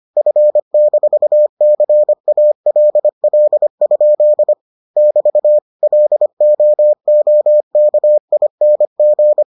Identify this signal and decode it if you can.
Morse code, heard as a keyed tone, about 25 wpm, 600 Hz. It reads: F-CALL? <BT> LOOKING